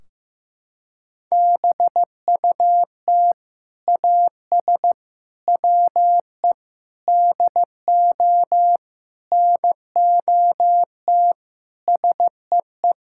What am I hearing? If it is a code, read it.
Morse code, heard as a keyed tone, about 15 wpm, 700 Hz. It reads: BUT AS WE DO NOT SEE
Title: Sentences from Top 1600 Words 15wpm